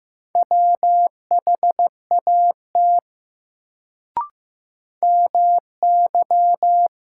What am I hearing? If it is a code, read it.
Morse code, heard as a keyed tone, about 15 wpm, 700 Hz. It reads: WHAT MY
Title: Sets of 2 Words - New Word Drill - 15wpm